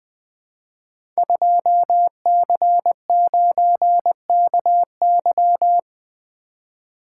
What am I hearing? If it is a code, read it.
Morse code, heard as a keyed tone, about 20 wpm, 700 Hz. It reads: 2C9KY
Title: Code Groups - 5 Letter:Numbers - 20wpm